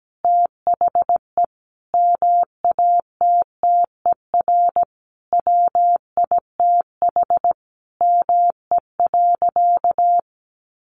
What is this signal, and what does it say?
Morse code, heard as a keyed tone, about 17 wpm, 700 Hz. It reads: THE MATTER WITH ME.
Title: Sentences from Top 700 Words 17wpm